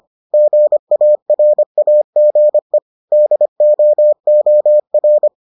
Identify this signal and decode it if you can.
Morse code, heard as a keyed tone, about 25 wpm, 600 Hz. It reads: GARAGE DOOR